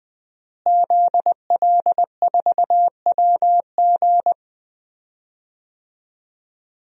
Morse code, heard as a keyed tone, about 20 wpm, 700 Hz. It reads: ZL4WG
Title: DXCC-100 Call Signs - 5 Characters - Repeated 3x - 20wpm